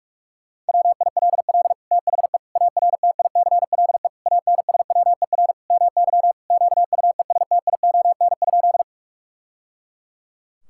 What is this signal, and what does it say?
Morse code, heard as a keyed tone, about 42 wpm, 700 Hz. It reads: WILL THE ARTICLE ANSWER MY QUESTION?
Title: Sentences from Top 1000 Words - Review Part 5 - 45wpm